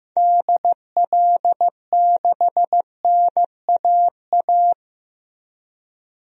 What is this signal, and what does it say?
Morse code, heard as a keyed tone, about 15 wpm, 700 Hz. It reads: DL6NAA
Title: DXCC-100 Call Signs - 6 Characters - Repeated 3x - 15wpm